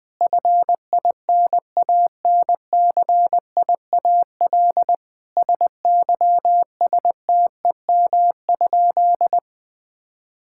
Morse code, heard as a keyed tone, about 21 wpm, 700 Hz. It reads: FINANCIAL SYSTEM?
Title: Sentences from Top 1000 Words - Review Part 5 - 20wpm